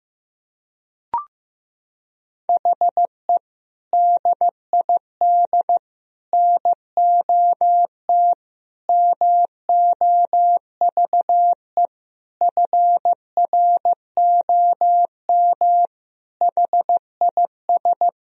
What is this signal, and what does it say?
Morse code, heard as a keyed tone, about 15 wpm, 700 Hz. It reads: HE DID NOT MOVE FROM HIS
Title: Sentences from Top 1000 Words - Review Part 1 - 15wpm